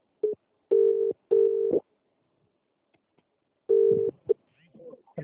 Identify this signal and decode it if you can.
Ringback tone (UK), 400 + 450 Hz.